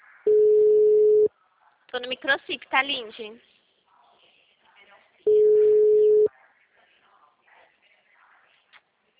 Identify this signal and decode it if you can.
Ringback tone (Europe, 425 Hz), with speech.
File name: out- -2041- - - .2207